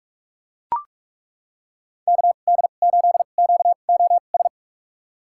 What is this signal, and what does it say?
Morse code, heard as a keyed tone, about 42 wpm, spaced as for 31 wpm, 700 Hz. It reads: KD8QOS
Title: USA Call Signs - 6 Characters - 2x Character Spacing - 45wpm